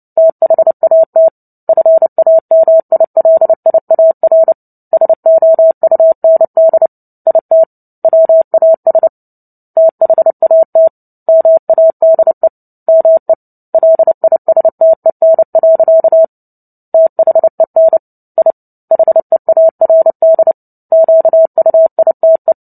Morse code, heard as a keyed tone, about 30 wpm, 650 Hz. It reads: THAT FAMILIAR SOUND IT WAS THAT MADE ME LISTEN. THEN I HEARD QUITE
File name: Book Two / 03-Chapter 3 - 29 WPM